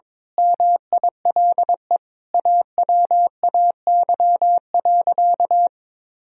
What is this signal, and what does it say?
Morse code, heard as a keyed tone, about 22 wpm, 700 Hz. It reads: MILE AWAY.
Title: Sentences from Top 1000 Words - Review Part 2 - 22wpm